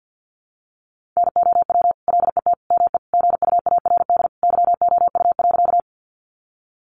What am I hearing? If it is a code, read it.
Morse code, heard as a keyed tone, about 50 wpm, 700 Hz. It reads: NOW LET ME GUARD YOU.
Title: Sentences from Top 700 Words 50wpm